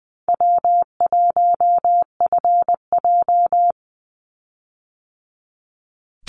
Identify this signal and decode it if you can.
Morse code, heard as a keyed tone, about 20 wpm, 700 Hz. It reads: W1FJ
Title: Easy Call Signs - 20wpm